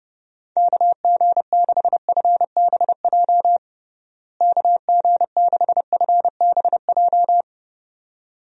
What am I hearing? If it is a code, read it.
Morse code, heard as a keyed tone, about 30 wpm, 700 Hz. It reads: KG6FBJ KG6FBJ
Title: USA Call Signs - 6 Characters - Repeated 2x - 30wpm